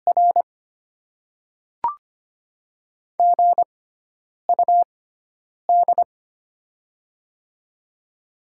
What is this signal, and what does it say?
Morse code, heard as a keyed tone, about 25 wpm, spaced as for 9 wpm, 700 Hz. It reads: R GUD
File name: Top 20 QSO Elements - 6x Character Spacing - 25wpm